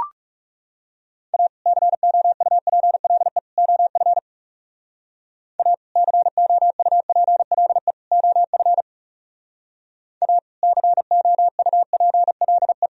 Morse code, heard as a keyed tone, about 37 wpm, 700 Hz. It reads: A COUPLE OF A COUPLE OF A COUPLE
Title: Speed-Racing - Sets of 3 Words 30wpm